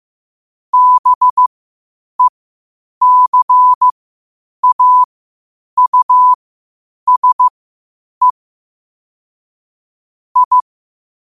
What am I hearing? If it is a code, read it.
Morse code, heard as a keyed tone, about 15 wpm, spaced as for 8 wpm, 1000 Hz. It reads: BECAUSE I
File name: qotdfiles / 8wpm